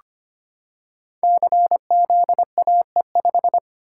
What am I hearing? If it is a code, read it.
Morse code, heard as a keyed tone, about 25 wpm, 700 Hz. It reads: CZAE5
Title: Code Groups - 5 Letter:Numbers - 25wpm